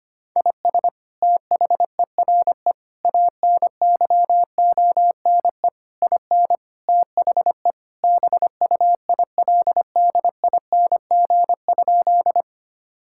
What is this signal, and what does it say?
Morse code, heard as a keyed tone, about 25 wpm, 700 Hz. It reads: IS THERE ANYONE IN THE BUILDING?
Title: Sentences from Top 1000 Words - Review Part 5 - 25wpm